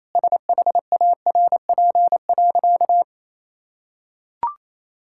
Morse code, heard as a keyed tone, about 28 wpm, 700 Hz. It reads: SHARP.
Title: Sentences from Top 1600 Words 28wpm